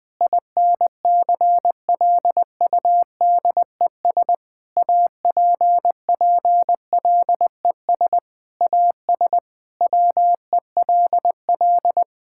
Morse code, heard as a keyed tone, about 20 wpm, 700 Hz. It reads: INCLUDES APPLES AS WELL
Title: Sentences with 7-Word Phrases - 20wpm